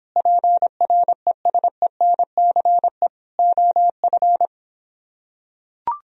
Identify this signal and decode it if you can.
Morse code, heard as a keyed tone, about 26 wpm, 700 Hz. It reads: PRESENCE OF